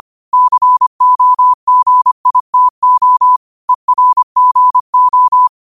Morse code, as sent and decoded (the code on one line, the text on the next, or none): -.-. --- --. .. - --- / . .-. --. ---
COGITO ERGO